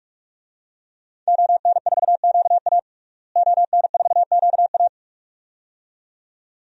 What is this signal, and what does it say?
Morse code, heard as a keyed tone, about 42 wpm, 700 Hz. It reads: ON4QA ON4QA
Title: DXCC-100 Call Signs - 5 Characters - Repeated 2x - 45wpm